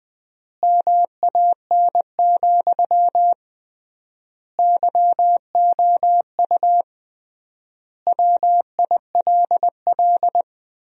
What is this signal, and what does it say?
Morse code, heard as a keyed tone, about 20 wpm, 700 Hz. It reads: MAN, YOU WILL
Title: 3x Word-Spacing - Sentences from Top 400 Words - 20wpm